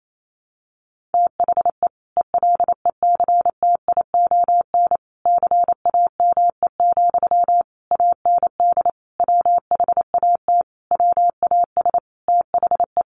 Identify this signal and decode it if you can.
Morse code, heard as a keyed tone, about 28 wpm, 700 Hz. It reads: THE ELECTION CAME, AND WHAT WAS THE
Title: Sentences from Top 1500 Words 28wpm